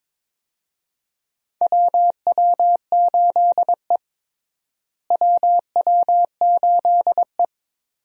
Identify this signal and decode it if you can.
Morse code, heard as a keyed tone, about 22 wpm, 700 Hz. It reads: WW8E WW8E
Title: USA Call Signs - 4 Characters - Repeated 2x - 22wpm